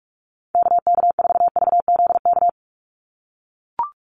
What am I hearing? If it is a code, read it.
Morse code, heard as a keyed tone, about 44 wpm, 700 Hz. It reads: KK4VZK
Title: USA Call Signs 45wpm